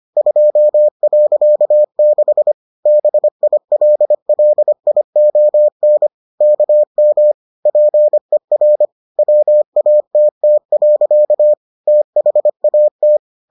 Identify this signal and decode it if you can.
Morse code, heard as a keyed tone, about 25 wpm, 600 Hz. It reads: 2.6 BILLION KM PER WATT. THAT